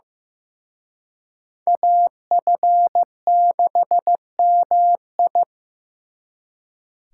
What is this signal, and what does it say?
Morse code, heard as a keyed tone, about 18 wpm, 700 Hz. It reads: AF6MI
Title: Speed Racing - USA 2x2 Call Signs 15wpm